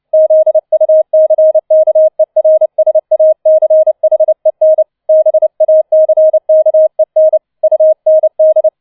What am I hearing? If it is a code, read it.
Morse code, heard as a keyed tone, about 29 wpm, 630 Hz. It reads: ZUCKERSACHEN BACKEN UND
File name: 006 Sechster Streich 30wpm